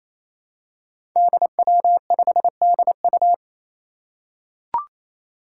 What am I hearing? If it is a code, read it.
Morse code, heard as a keyed tone, about 27 wpm, 700 Hz. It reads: DW5DU
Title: Five Challenging Letter-Number Combinations 28wpm